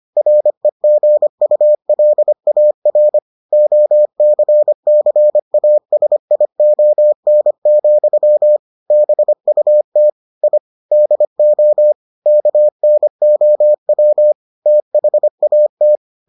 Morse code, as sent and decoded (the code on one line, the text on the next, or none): .-. --. ..- .-.. .- .-. / --- -.-. -.-. .- ... .. --- -. --..-- / -... ..- - / .. / -.. --- / -.- -. --- .-- / - .... .- -
REGULAR OCCASION, BUT I DO KNOW THAT